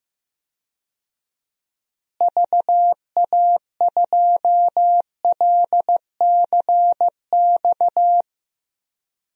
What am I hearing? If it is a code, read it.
Morse code, heard as a keyed tone, about 15 wpm, 700 Hz. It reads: VA2LCX
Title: DXCC-100 Call Signs - 6 Characters - Repeated 2x - 15wpm